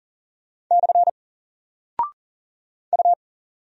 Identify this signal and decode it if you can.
Morse code, heard as a keyed tone, about 40 wpm, 700 Hz. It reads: / U
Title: Single Character (Letter:Number:Punctuation:Prosign) Alt 40wpm